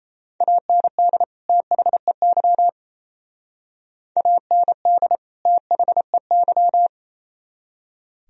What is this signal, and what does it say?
Morse code, heard as a keyed tone, about 36 wpm, 700 Hz. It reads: AND THEY AND THEY